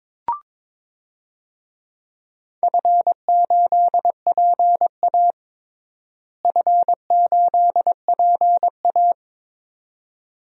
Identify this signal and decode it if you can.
Morse code, heard as a keyed tone, about 22 wpm, 700 Hz. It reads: F8PA F8PA
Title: DXCC-100 Call Signs - 4 Characters - Repeated 2x - 22wpm